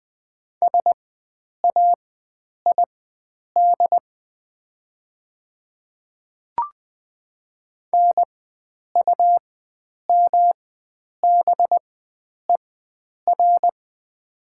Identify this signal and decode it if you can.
Morse code, heard as a keyed tone, about 20 wpm, spaced as for 9 wpm, 700 Hz. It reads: SAID NUMBER
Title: 4x Character Spacing - Top 100 Words - 20wpm